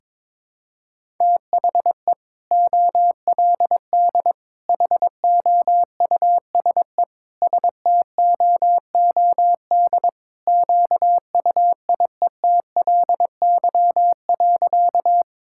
Morse code, heard as a keyed tone, about 22 wpm, 700 Hz. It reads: THE OLD HOUSE STOOD QUIETLY.
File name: practice / 22wpm / Sentences with 2-Word Phrases - 22wpm